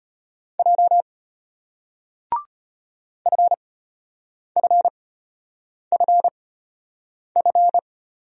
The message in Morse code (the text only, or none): J FFFF